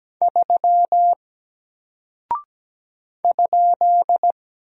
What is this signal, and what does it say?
Morse code, heard as a keyed tone, about 17 wpm, 700 Hz. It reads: 3 ?
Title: Single Character (Letter:Number:Punctuation:Prosign) Alt 17wpm